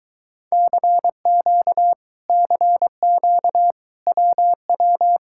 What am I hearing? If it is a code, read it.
Morse code, heard as a keyed tone, about 23 wpm, 700 Hz. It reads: CQ CQ WW